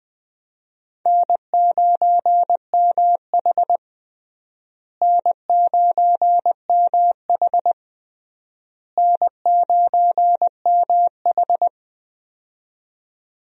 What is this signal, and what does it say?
Morse code, heard as a keyed tone, about 20 wpm, 700 Hz. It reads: N9MH N9MH N9MH
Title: USA Call Signs - 4 Characters - Repeated 3x - 20wpm